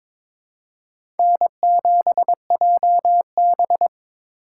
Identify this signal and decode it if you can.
Morse code, heard as a keyed tone, about 22 wpm, 700 Hz. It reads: N7JB
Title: USA Call Signs - 4 Characters - 4x Character Spacing - 22wpm